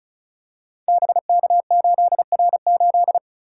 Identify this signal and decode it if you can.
Morse code, heard as a keyed tone, about 36 wpm, 700 Hz. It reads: BK8R8
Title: Code Groups - 5 Letter:Numbers - 35wpm